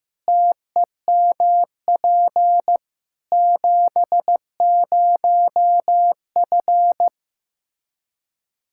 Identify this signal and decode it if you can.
Morse code, heard as a keyed tone, about 15 wpm, 700 Hz. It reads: TEMP 70F
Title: Practice QSO - Exchange 2 - Info - 15wpm